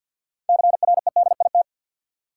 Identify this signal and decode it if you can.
Morse code, heard as a keyed tone, about 52 wpm, 700 Hz. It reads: CREDIT
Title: Sentences from Top 1000 Words - New Word Drill - 50wpm